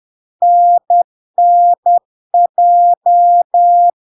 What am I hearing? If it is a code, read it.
Morse code, heard as a keyed tone, about 10 wpm, 700 Hz. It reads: NNJ
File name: LARK-FD-2022-Log-in-CW-10wpm-700-Hz